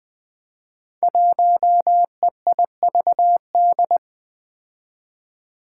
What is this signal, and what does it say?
Morse code, heard as a keyed tone, about 21 wpm, 700 Hz. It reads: 1EIVD
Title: Five Challenging Letter-Number Combinations 20wpm